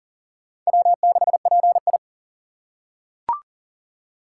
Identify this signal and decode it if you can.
Morse code, heard as a keyed tone, about 40 wpm, 700 Hz. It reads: W6PI
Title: USA Call Signs - 2x Character Spacing - 40wpm